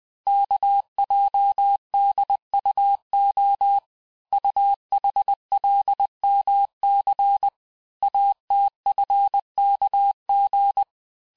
Morse code, heard as a keyed tone, about 20 wpm, 800 Hz. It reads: KJDUO UHLMC ATFKG
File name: cours-858-texte-chiffre-vitesse-19mpm